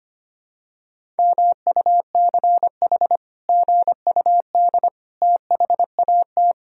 Morse code, heard as a keyed tone, about 25 wpm, 700 Hz. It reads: MUCH GUD THAT
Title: Sentences from Top 200 Words and Abbreviations 25wpm